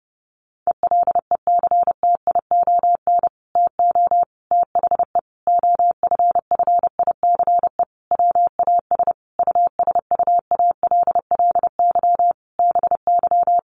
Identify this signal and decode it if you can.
Morse code, heard as a keyed tone, about 28 wpm, 700 Hz. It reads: ELECTION TO THE OFFICE WAS USUALLY BY
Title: Sentences from Top 1500 Words 30wpm